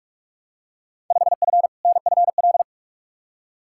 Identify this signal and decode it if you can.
Morse code, heard as a keyed tone, about 46 wpm, 700 Hz. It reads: 5F NFL
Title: Field Day Exchanges Encore - 45wpm